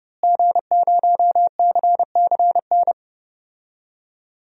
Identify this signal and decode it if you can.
Morse code, heard as a keyed tone, about 30 wpm, 700 Hz. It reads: G0CCN
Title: DXCC-100 Call Signs - 5 Characters - Repeated 3x - 30wpm